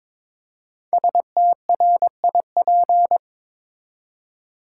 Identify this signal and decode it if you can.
Morse code, heard as a keyed tone, about 22 wpm, 700 Hz. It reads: STRIP
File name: 5 Letter Words - First 3 Letters Given - 22wpm